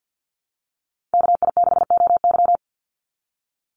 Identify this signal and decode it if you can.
Morse code, heard as a keyed tone, about 48 wpm, 700 Hz. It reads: KI6OY
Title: Call Signs Repeated 3x - 50wpm